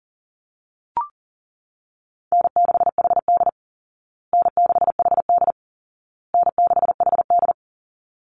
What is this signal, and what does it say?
Morse code, heard as a keyed tone, about 40 wpm, 700 Hz. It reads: N6HD N6HD N6HD
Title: Easy Call Signs Repeated 3x - 40wpm